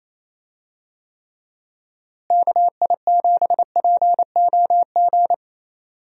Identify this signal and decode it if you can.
Morse code, heard as a keyed tone, about 28 wpm, 700 Hz. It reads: KI7POG
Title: USA Call Signs - 6 Characters - Repeated 3x - 28wpm